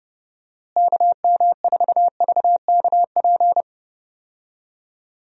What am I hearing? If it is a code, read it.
Morse code, heard as a keyed tone, about 30 wpm, 700 Hz. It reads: KM4VKP
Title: USA Call Signs - 6 Characters - Repeated 2x - 30wpm